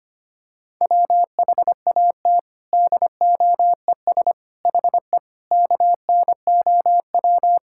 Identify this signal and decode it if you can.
Morse code, heard as a keyed tone, about 25 wpm, 700 Hz. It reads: WHAT DOES HE KNOW
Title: Sentences from Top 1600 Words 25wpm